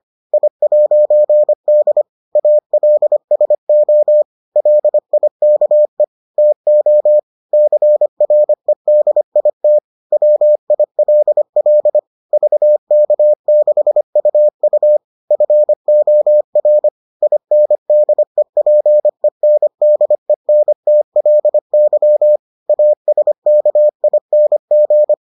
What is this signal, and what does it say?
Morse code, heard as a keyed tone, about 25 wpm, 600 Hz. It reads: I'D ALSO LIKE TO CREDIT WILL VK6UU FOR INDEPENDENTLY ASKING